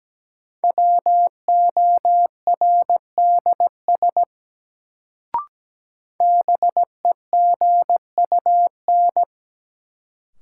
Morse code, heard as a keyed tone, about 17 wpm, 700 Hz. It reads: WORDS BEGUN
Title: Sentences from Top 1000 Words - Review Part 6 - New Word Drill - 17wpm